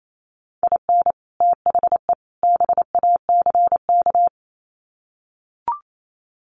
Morse code, heard as a keyed tone, about 28 wpm, 700 Hz. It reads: IN THE BACK
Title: Sets of 3 Words 28wpm